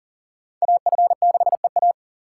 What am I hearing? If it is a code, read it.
Morse code, heard as a keyed tone, about 43 wpm, 690 Hz. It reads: AF6EA
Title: USA Call Signs - 2x Character Spacing - 40wpm